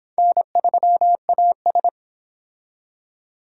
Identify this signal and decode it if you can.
Morse code, heard as a keyed tone, about 25 wpm, 700 Hz. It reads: N3AS